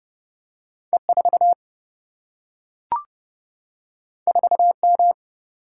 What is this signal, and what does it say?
Morse code, heard as a keyed tone, about 30 wpm, 700 Hz. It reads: E 4 4 M
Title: Single-Letter Single-Number 30wpm